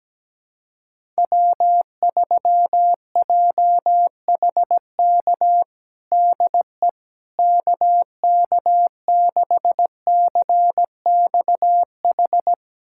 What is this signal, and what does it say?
Morse code, heard as a keyed tone, about 17 wpm, 700 Hz. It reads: W3JHK DE KK6CXH
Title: USA Callsign DE Callsign 17wpm